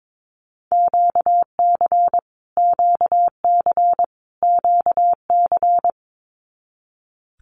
Morse code, heard as a keyed tone, about 22 wpm, 700 Hz. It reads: QC QC QC
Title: Canadian Provinces and Territories - Repeated 3x - 22wpm